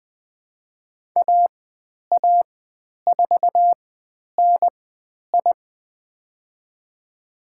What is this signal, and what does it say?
Morse code, heard as a keyed tone, about 20 wpm, spaced as for 10 wpm, 700 Hz. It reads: AA4NI
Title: 102 Lesson 17g - Letter u - Call Signs 20wpm